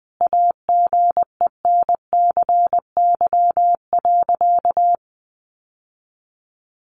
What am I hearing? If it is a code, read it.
Morse code, heard as a keyed tone, about 19 wpm, 700 Hz. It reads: AGENCY.
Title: Sentences from Top 1300 Words 20wpm